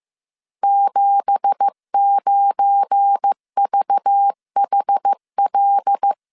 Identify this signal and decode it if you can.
Morse code, heard as a keyed tone, about 15 wpm, 800 Hz. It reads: 79VHL
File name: CW / DET / 1 9 1500C S C 14wpm 3245